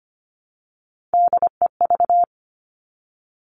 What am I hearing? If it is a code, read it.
Morse code, heard as a keyed tone, about 25 wpm, 700 Hz. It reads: DEV
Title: Three Challenging Letter-Number Combinations 25wpm